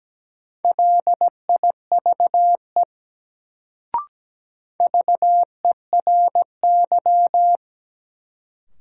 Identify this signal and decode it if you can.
Morse code, heard as a keyed tone, about 17 wpm, 700 Hz. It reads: LIVE VERY
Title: Sentences from Top 200 Words - New Word Drill - 17wpm